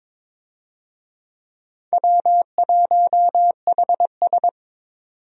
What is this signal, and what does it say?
Morse code, heard as a keyed tone, about 22 wpm, 700 Hz. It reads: W1HS
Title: USA Call Signs - 4 Characters - 22wpm